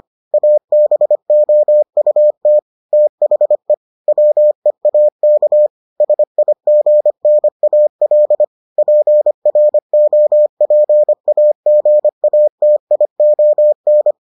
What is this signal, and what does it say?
Morse code, heard as a keyed tone, about 25 wpm, 600 Hz. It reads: ABOUT THE WEAK SIGNAL PROPAGATION